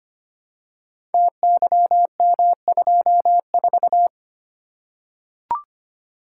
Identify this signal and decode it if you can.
Morse code, heard as a keyed tone, about 25 wpm, 700 Hz. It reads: TYM24
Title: Code Groups - 5 Letter:Numbers - 25wpm